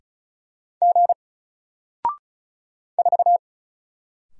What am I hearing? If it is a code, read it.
Morse code, heard as a keyed tone, about 36 wpm, 700 Hz. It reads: G 4
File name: Single-Letter-Number-35wpm